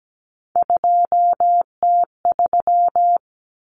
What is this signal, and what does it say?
Morse code, heard as a keyed tone, about 17 wpm, 700 Hz. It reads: 2T3
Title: Common Cut Numbers - 3-Digit 17wpm